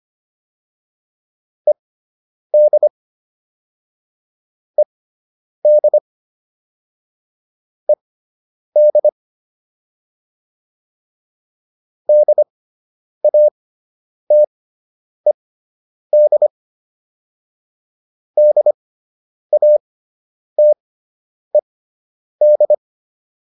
Morse code, heard as a keyed tone, about 25 wpm, spaced as for 9 wpm, 600 Hz. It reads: ED ED ED DATED DATED
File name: ED5 09